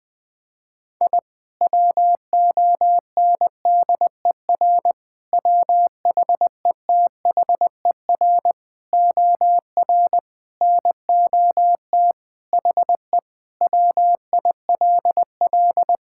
Morse code, heard as a keyed tone, about 20 wpm, 700 Hz. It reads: I WONDER WHETHER OR NOT HE WILL
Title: Sentences with 3-Word Phrases - 20wpm